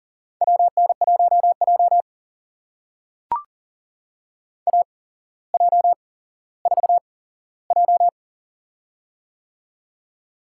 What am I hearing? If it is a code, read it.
Morse code, heard as a keyed tone, about 40 wpm, spaced as for 17 wpm, 700 Hz. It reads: WN1J AJ4J
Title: USA Call Signs - 4 Characters - 8x Character Spacing - 40wpm